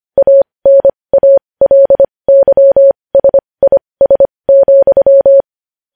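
Morse code, heard as a keyed tone, about 25 wpm, spaced as for 20 wpm, 550 Hz. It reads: ANALYSIS,
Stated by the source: News headlines sent in Morse code at 20.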